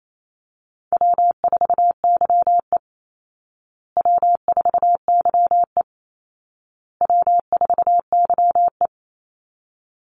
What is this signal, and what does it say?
Morse code, heard as a keyed tone, about 28 wpm, 700 Hz. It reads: W4YE W4YE W4YE
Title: Call Signs Repeated 2x - 28wpm